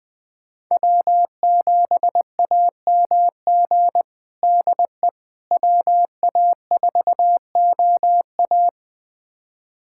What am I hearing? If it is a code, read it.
Morse code, heard as a keyed tone, about 20 wpm, 700 Hz. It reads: W7AMG DE WA4OA
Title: USA Callsign DE Callsign - 5 Characters - 20wpm